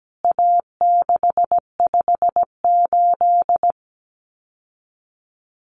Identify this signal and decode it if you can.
Morse code, heard as a keyed tone, about 17 wpm, 700 Hz. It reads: A658
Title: CWT Exchange 17wpm